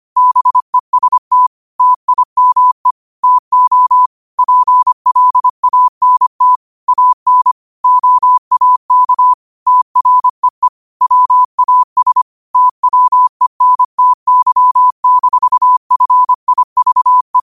Morse code, heard as a keyed tone, about 25 wpm, 1000 Hz. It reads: BEST TIME TO PLANT AN OAK TREE WAS TWENTY-FIVE
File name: qotdfiles / 25wpm